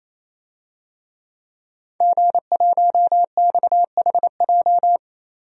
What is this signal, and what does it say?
Morse code, heard as a keyed tone, about 28 wpm, 700 Hz. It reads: G1XHJ
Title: DXCC-100 Call Signs - 5 Characters - Repeated 3x - 28wpm